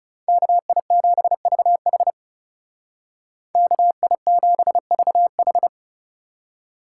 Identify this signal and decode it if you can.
Morse code, heard as a keyed tone, about 40 wpm, 700 Hz. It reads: KI7VH KI7VH